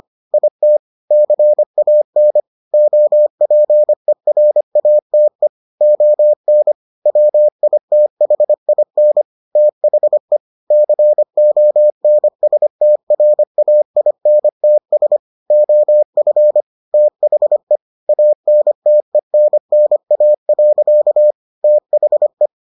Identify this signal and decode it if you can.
Morse code, heard as a keyed tone, about 25 wpm, 600 Hz. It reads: IT CAN OPERATE ON WITHIN THE CONSTRAINTS OF THE ANTENNA. THE